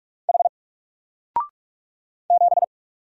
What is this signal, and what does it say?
Morse code, heard as a keyed tone, about 46 wpm, 700 Hz. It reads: H 7
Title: Single Letter-Number 45wpm